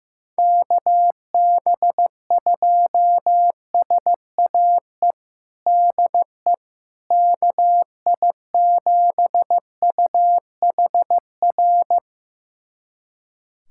Morse code, heard as a keyed tone, about 16 wpm, 700 Hz. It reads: KB2SAE DE KI7UHR
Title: USA Callsign DE Callsign - 6 Characters - 15wpm